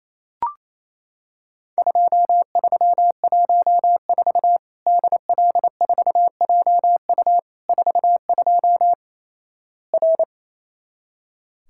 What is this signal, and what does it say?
Morse code, heard as a keyed tone, about 27 wpm, 700 Hz. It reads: 2314 DL4JU 42 R
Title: Worked All Europe DX Contest - QTC Exchange - 28wpm